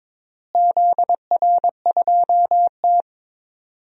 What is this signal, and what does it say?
Morse code, heard as a keyed tone, about 22 wpm, 700 Hz. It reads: ZR2T
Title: DXCC-100 Call Signs - 4 Characters - Repeated 3x - 22wpm